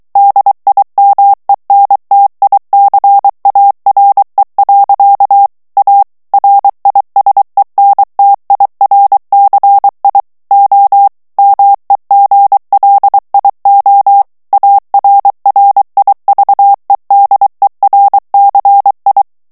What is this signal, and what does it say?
Morse code, heard as a keyed tone, about 23 wpm, 800 Hz. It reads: DIMENTICARE. A RISENTIRCI O MEGLIO ARRIVEDERCI
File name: morse mp3 / ITALIEN texte clair 1 vitesse 22 mots minute